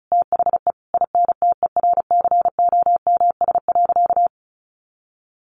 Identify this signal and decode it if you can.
Morse code, heard as a keyed tone, about 34 wpm, 700 Hz. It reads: THE INTERCOMS.
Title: A Sound Of Thunder 35wpm